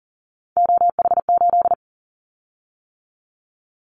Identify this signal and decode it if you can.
Morse code, heard as a keyed tone, about 40 wpm, 700 Hz. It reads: OH8
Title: Callsign Prefixes for Top 25 DXCC - 40wpm